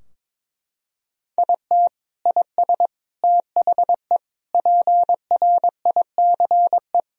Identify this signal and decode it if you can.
Morse code, heard as a keyed tone, about 22 wpm, 700 Hz. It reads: IT IS THE PRICE
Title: Sentences from Top 1000 Words - Review Part 4 - 22wpm